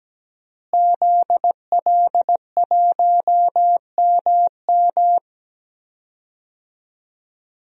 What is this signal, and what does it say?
Morse code, heard as a keyed tone, about 17 wpm, 700 Hz. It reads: ZL1MM
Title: DXCC-100 Call Signs - 5 Characters - Repeated 3x - 17wpm